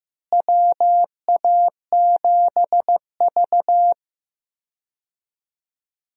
Morse code, heard as a keyed tone, about 15 wpm, 700 Hz. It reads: WA7V
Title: USA Call Signs - 4 Characters - Repeated 2x - 15wpm